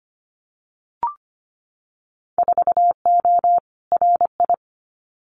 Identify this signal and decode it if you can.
Morse code, heard as a keyed tone, about 25 wpm, 700 Hz. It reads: 4O RI
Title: Winter Field Day Exchanges Encore 25wpm